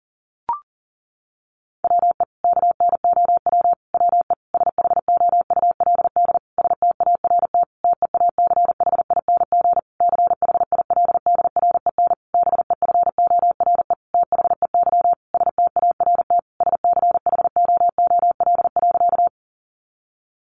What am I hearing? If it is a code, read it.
Morse code, heard as a keyed tone, about 40 wpm, 700 Hz. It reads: WE KNOW WE SHOULD START TEACHING CHILDREN BEFORE THEY START SCHOOL.
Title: Sentences from Top 1100 Words 40wpm